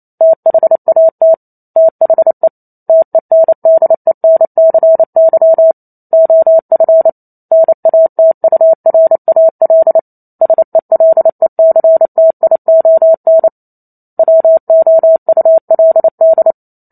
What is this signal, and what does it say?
Morse code, heard as a keyed tone, about 28 wpm, 650 Hz. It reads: THAT THE TENDENCY OF NATURAL SELECTION WOULD
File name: Chapter 2 - 28 WPM